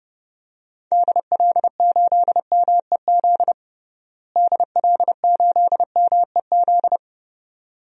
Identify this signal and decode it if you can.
Morse code, heard as a keyed tone, about 32 wpm, 700 Hz. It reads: DL8MEZ DL8MEZ
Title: DXCC-100 Call Signs - 6 Characters - Repeated 2x - 30wpm